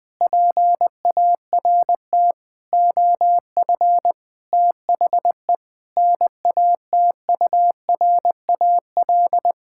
Morse code, heard as a keyed tone, about 20 wpm, 700 Hz. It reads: PART OF THE NATURAL
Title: Sentences from Top 1000 Words - Review Part 6 - 20wpm